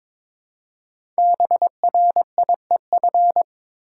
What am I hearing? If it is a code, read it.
Morse code, heard as a keyed tone, about 23 wpm, 700 Hz. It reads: BRIEF
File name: 5 Letter Words - First 3 Letters Given - 22wpm